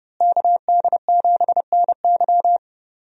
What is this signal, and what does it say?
Morse code, heard as a keyed tone, about 29 wpm, 700 Hz. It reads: KD7NY
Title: USA Call Signs - 5 Characters - Repeated 3x - 30wpm